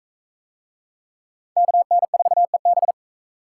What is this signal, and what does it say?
Morse code, heard as a keyed tone, about 44 wpm, 700 Hz. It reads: KN4EB